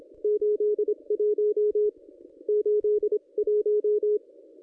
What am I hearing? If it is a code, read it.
Morse code, heard as a keyed tone, about 24 wpm, 440 Hz. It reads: 81 81
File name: contests / ss-cw-k5tr-op-n5rz -1400